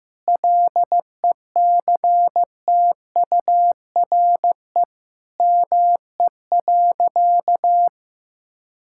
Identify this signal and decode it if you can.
Morse code, heard as a keyed tone, about 15 wpm, 700 Hz. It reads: LECTURE ME.
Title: Sentences from Top 1600 Words 15wpm